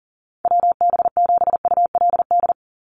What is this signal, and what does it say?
Morse code, heard as a keyed tone, about 40 wpm, 700 Hz. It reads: WB7ULD
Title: NA SOTA Chaser Callsigns 40wpm